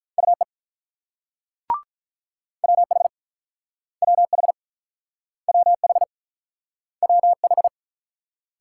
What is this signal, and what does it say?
Morse code, heard as a keyed tone, about 47 wpm, 700 Hz. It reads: UE WH WH WH WH